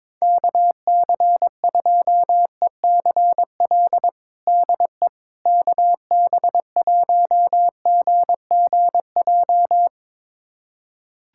Morse code, heard as a keyed tone, about 22 wpm, 700 Hz. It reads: KC2ECL DE KB1GGJ
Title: USA Callsign DE Callsign - 6 Characters - 22wpm